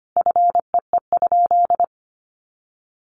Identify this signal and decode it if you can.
Morse code, heard as a keyed tone, about 25 wpm, 700 Hz. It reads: FEE?
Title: Sentences from Top 1500 Words 25wpm